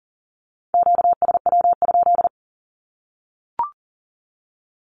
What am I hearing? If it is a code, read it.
Morse code, heard as a keyed tone, about 40 wpm, 700 Hz. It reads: QSW?
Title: Q-Codes 40wpm